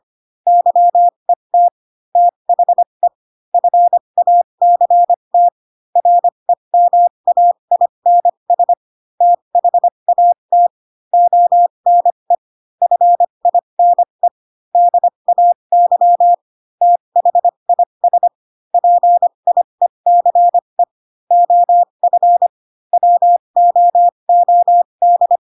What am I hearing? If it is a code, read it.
Morse code, heard as a keyed tone, about 25 wpm, spaced as for 22 wpm, 700 Hz. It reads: YET THE FACT REMAINS THAT ONE FINE DAY THIS PIECE OF WOOD